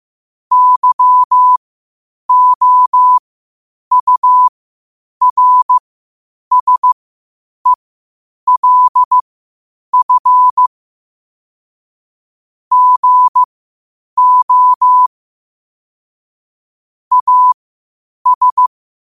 Morse code, heard as a keyed tone, about 15 wpm, spaced as for 8 wpm, 1000 Hz. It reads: YOURSELF GO AS
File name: qotdfiles / 8wpm